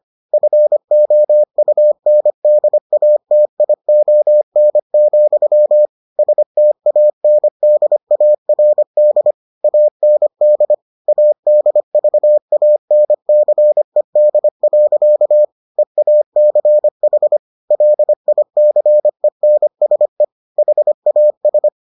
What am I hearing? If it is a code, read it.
Morse code, heard as a keyed tone, about 25 wpm, 600 Hz. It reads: FOUNDATION, STANDARD AND ADVANCED. EACH LICENSE HAS